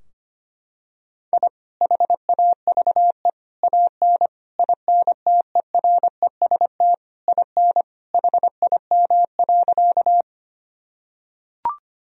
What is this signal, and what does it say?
Morse code, heard as a keyed tone, about 25 wpm, 700 Hz. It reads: I HAVE AN INTEREST IN HIM.
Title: Sentences from Top 1000 Words - Review Part 2 - 25wpm